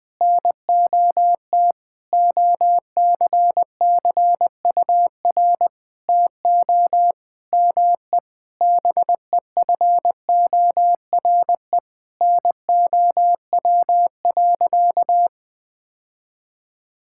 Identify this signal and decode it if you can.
Morse code, heard as a keyed tone, about 19 wpm, 700 Hz. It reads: NOT OCCUR TO ME BEFORE NOW.
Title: Sentences from Top 1000 Words - Review Part 6 - 20wpm